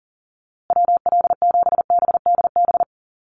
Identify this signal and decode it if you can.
Morse code, heard as a keyed tone, about 40 wpm, 700 Hz. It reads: WL7BDB
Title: USA Call Signs Repeated 2x 40wpm